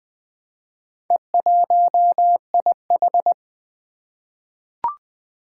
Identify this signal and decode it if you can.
Morse code, heard as a keyed tone, about 20 wpm, 700 Hz. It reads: E1IH
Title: Four Challenging Letter-Number Combinations 20wpm